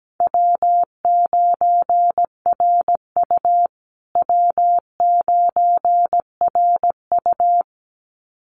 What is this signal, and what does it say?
Morse code, heard as a keyed tone, about 17 wpm, 700 Hz. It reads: W9RU W9RU
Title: USA Call Signs Repeated 2x 17wpm